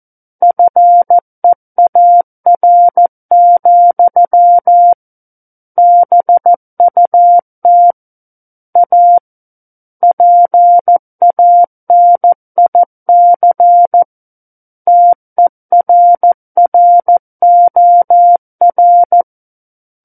Morse code, heard as a keyed tone, about 14 wpm, 700 Hz. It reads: FEAR, BUT A PANIC TERROR
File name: Book One / 05-Chapter 5 - 14 WPM